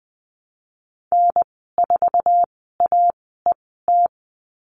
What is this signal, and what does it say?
Morse code, heard as a keyed tone, about 20 wpm, spaced as for 15 wpm, 700 Hz. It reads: N4AET
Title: 147 Lesson 24g - Letter y - Call Signs 20wpm